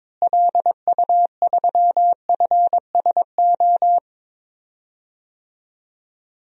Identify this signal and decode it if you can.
Morse code, heard as a keyed tone, about 22 wpm, 700 Hz. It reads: LU3FSO
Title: DXCC-100 Call Signs - 6 Characters - Repeated 2x - 22wpm